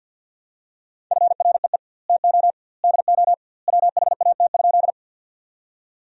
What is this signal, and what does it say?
Morse code, heard as a keyed tone, about 50 wpm, 700 Hz. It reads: FREE TO DO WHAT?
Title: 2x Word-Spacing - Sentences from Top 500 Words - 50wpm